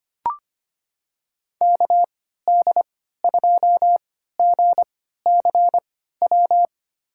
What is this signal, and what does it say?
Morse code, heard as a keyed tone, about 25 wpm, spaced as for 14 wpm, 700 Hz. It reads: KD2GCW
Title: USA Call Signs - 6 Characters - 3x Character Spacing - 25wpm